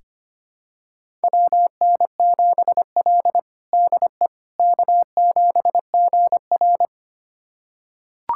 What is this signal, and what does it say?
Morse code, heard as a keyed tone, about 25 wpm, 700 Hz. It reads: WN7L DE K7GR
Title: USA Callsign DE Callsign - 4 Characters - 25wpm